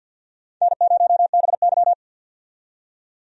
Morse code, heard as a keyed tone, about 47 wpm, 700 Hz. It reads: N0BY
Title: USA Call Signs - 4 Characters - 50wpm